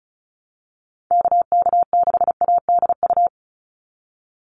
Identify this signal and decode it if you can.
Morse code, heard as a keyed tone, about 36 wpm, 700 Hz. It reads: KK6ADU
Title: USA Call Signs 35wpm